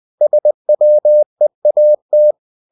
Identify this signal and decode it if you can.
Morse code, heard as a keyed tone, about 20 wpm, 600 Hz. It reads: SWEAT
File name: the old man and the sea / the old man and the sea part 1 20wpm